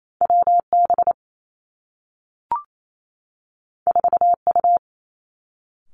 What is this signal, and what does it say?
Morse code, heard as a keyed tone, about 28 wpm, 700 Hz. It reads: W B 4 U
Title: Two Challenging Letter-Number Combinations 28wpm